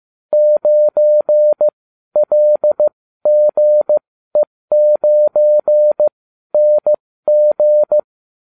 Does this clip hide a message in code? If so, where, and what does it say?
Yes: Morse code from 0.3 to 8.0 s, heard as a keyed tone, about 15 wpm, spaced as for 12 wpm, 600 Hz. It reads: 9LGE9NG